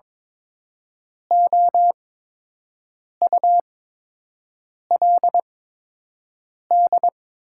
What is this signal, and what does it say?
Morse code, heard as a keyed tone, about 22 wpm, spaced as for 6 wpm, 700 Hz. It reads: OULD
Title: 8x Character Spacing - Top 500 Words - 22wpm